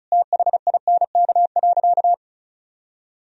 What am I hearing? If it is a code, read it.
Morse code, heard as a keyed tone, about 34 wpm, 700 Hz. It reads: THINK.
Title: Sentences from Top 400 Words and Abbreviations 35wpm